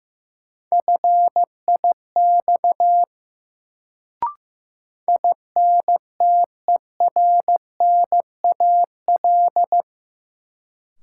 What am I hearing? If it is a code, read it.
Morse code, heard as a keyed tone, about 16 wpm, 700 Hz. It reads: FIX INTERNAL
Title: Sentences from Top 1400 Words - New Word Drill - 15wpm